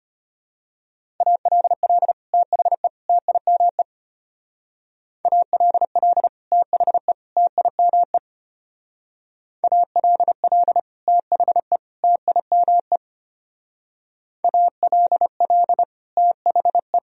Morse code, heard as a keyed tone, about 32 wpm, 700 Hz. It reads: ALL THE TIME ALL THE TIME ALL THE TIME ALL THE